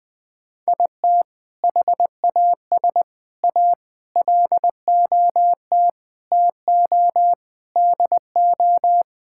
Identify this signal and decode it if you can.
Morse code, heard as a keyed tone, about 21 wpm, 700 Hz. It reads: IT HAS A LOT TO DO
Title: Sentences with 5-Word Phrases - 20wpm